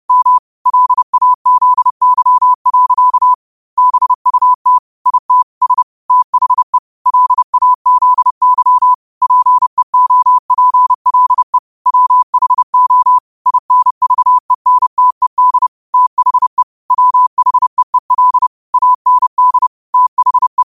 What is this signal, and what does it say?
Morse code, heard as a keyed tone, about 30 wpm, 1000 Hz. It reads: M LAZY. BUT IT S THE LAZY PEOPLE WHO INVENTED THE WHEEL AND THE
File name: qotdfiles / 30wpm